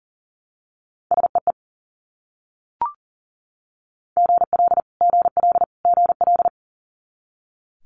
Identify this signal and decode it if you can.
Morse code, heard as a keyed tone, about 37 wpm, 690 Hz. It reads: SEE GL GL GL
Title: Multi-letter Phonemes Repeated 3x 40wpm